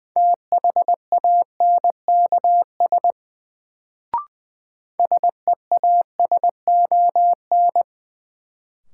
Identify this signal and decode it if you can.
Morse code, heard as a keyed tone, about 20 wpm, 700 Hz. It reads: THANKS SEASON
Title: Sets of 5 Words - New Word Drill - 20wpm